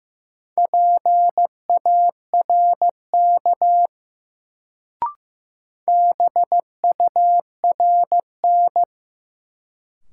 Morse code, heard as a keyed tone, about 15 wpm, 700 Hz. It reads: PARK BURN
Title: Sentences from Top 1300 Words - New Word Drill Alt - 15wpm